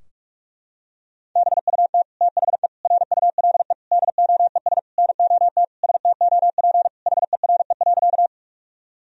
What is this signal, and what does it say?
Morse code, heard as a keyed tone, about 41 wpm, 700 Hz. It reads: BUT THE RULE DOES NOT STOP HERE.
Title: Sentences from Top 1000 Words - Review Part 2 - 45wpm